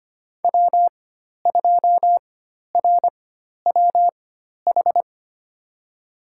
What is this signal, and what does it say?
Morse code, heard as a keyed tone, about 25 wpm, spaced as for 12 wpm, 700 Hz. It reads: W2RWH
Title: USA Call Signs - 5 Characters - 4x Character Spacing - 25wpm